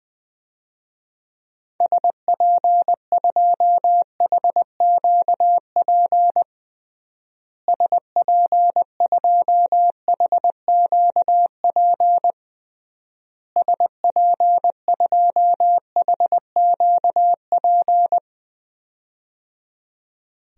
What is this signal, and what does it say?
Morse code, heard as a keyed tone, about 21 wpm, 700 Hz. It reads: SP2HQP SP2HQP SP2HQP
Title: DXCC-100 Call Signs - 6 Characters - Repeated 3x - 20wpm